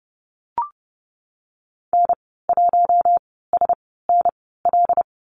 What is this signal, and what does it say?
Morse code, heard as a keyed tone, about 30 wpm, spaced as for 17 wpm, 700 Hz. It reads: N1SNL
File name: Lesson 15g - Number 5 - Call Signs 30wpm